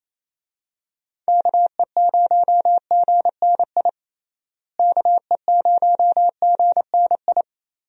Morse code, heard as a keyed tone, about 27 wpm, 700 Hz. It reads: KE0GNI KE0GNI
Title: USA Call Signs - 6 Characters - Repeated 2x - 28wpm